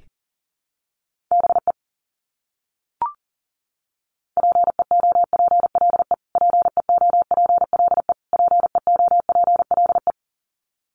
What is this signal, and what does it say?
Morse code, heard as a keyed tone, about 40 wpm, 700 Hz. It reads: BE PEOPLE PEOPLE PEOPLE
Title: Top 300 Words Repeated 40wpm